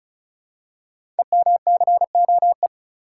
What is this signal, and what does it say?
Morse code, heard as a keyed tone, about 36 wpm, 700 Hz. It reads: EMCOE
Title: Code Groups - 5 Letter:Numbers - 35wpm